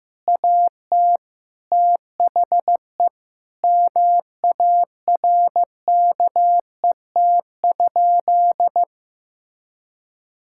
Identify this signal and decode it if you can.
Morse code, heard as a keyed tone, about 16 wpm, 700 Hz. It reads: AT THE MARKET?
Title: Sentences from Top 1000 Words - Review Part 6 - 15wpm